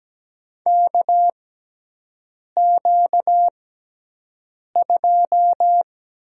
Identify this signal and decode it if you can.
Morse code, heard as a keyed tone, about 17 wpm, spaced as for 6 wpm, 700 Hz. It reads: KQ2
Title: USA Call Signs - 4 Characters - 6x Character Spacing - 17wpm